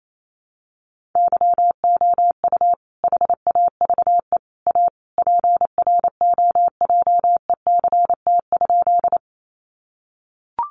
Morse code, heard as a keyed tone, about 28 wpm, 700 Hz. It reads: YOU HAVE A PROJECT?
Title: Sentences from Top 1400 Words 28wpm